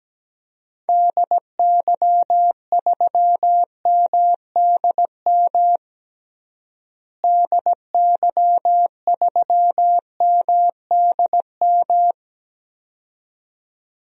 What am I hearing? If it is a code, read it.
Morse code, heard as a keyed tone, about 17 wpm, 700 Hz. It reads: DY3MDM DY3MDM
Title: DXCC-100 Call Signs - 6 Characters - Repeated 2x - 17wpm